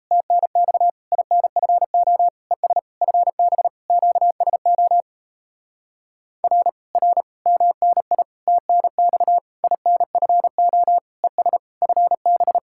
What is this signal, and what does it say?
Morse code, heard as a keyed tone, about 37 wpm, 700 Hz. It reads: TNX INFO ES FB QSO R R MNI TNX INFO ES FB